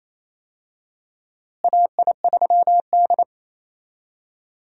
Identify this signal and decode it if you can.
Morse code, heard as a keyed tone, about 27 wpm, 700 Hz. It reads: AI3D
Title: USA Call Signs - 4 Characters - Repeated 3x - 28wpm